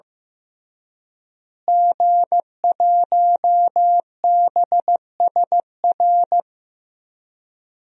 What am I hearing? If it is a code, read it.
Morse code, heard as a keyed tone, about 15 wpm, 700 Hz. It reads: G1BSR
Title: DXCC-100 Call Signs - 5 Characters - Repeated 3x - 15wpm